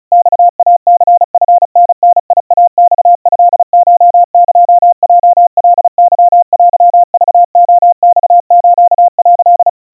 Morse code (Japanese Wabun code), heard as a keyed tone, about 36 wpm, 700 Hz. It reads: マイニチタダイマトコエヲカケテクレマス」
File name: kawa / Morse sample / kamyu 130 cpm 700 Hz Crispy